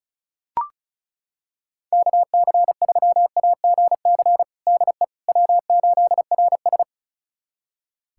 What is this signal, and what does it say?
Morse code, heard as a keyed tone, about 36 wpm, 700 Hz. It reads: KC3AGC DE W8RS
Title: USA Callsign DE Callsign 35wpm